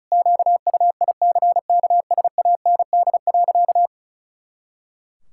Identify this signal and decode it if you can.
Morse code, heard as a keyed tone, about 36 wpm, 700 Hz. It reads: QUICKSAND.
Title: Pangram Challenge v2 - 35wpm